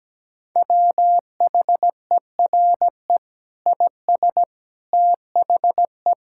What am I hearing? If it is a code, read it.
Morse code, heard as a keyed tone, about 17 wpm, 700 Hz. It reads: WHERE IS THE
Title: Sentences from Top 1600 Words 17wpm